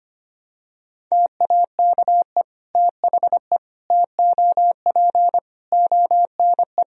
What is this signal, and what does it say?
Morse code, heard as a keyed tone, about 25 wpm, 700 Hz. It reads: TAKE THE TOP ONE
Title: Sentences from Top 1000 Words - Review Part 2 - 25wpm